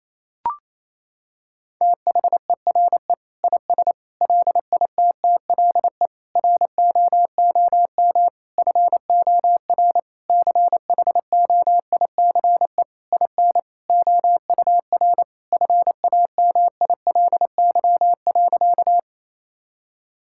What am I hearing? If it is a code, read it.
Morse code, heard as a keyed tone, about 28 wpm, 700 Hz. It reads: THERE IS LITTLE ROOM FOR CHOICE IN OUR FAMILY.
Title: Sentences from Top 1000 Words - Review Part 6 - 28wpm